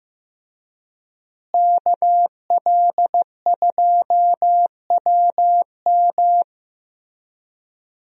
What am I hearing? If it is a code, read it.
Morse code, heard as a keyed tone, about 15 wpm, 700 Hz. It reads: KL2WM
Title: USA Call Signs - 5 Characters - Repeated 3x - 15wpm